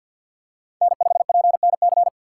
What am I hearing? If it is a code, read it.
Morse code, heard as a keyed tone, about 52 wpm, 700 Hz. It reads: N5PNC
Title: USA Call Signs - 5 Characters - 3x Character Spacing - 50wpm